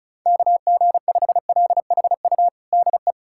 Morse code, heard as a keyed tone, about 36 wpm, 700 Hz. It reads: KG5LHU DE
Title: USA Callsign DE Callsign - 6 Characters - 35wpm